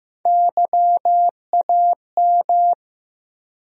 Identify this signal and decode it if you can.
Morse code, heard as a keyed tone, about 15 wpm, 700 Hz. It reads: YAM
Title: USA Call Sign Postfixes - 3 Letters 15wpm